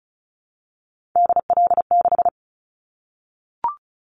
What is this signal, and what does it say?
Morse code, heard as a keyed tone, about 36 wpm, 700 Hz. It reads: DL6
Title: Callsign Prefixes for Top 25 DXCC - 35wpm